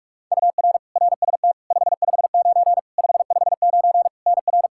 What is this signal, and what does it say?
Morse code, heard as a keyed tone, about 41 wpm, 700 Hz. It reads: UR RST 559 559 NR
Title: Practice QSO - Exchange 1 - Beginning - 45wpm